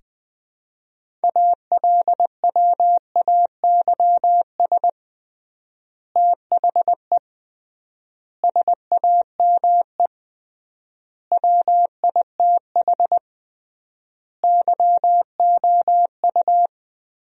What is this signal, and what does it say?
Morse code, heard as a keyed tone, about 21 wpm, 700 Hz. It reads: ALWAYS THE SAME WITH YOU
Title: 3x Word-Spacing - Sentences from Top 300 Words - 20wpm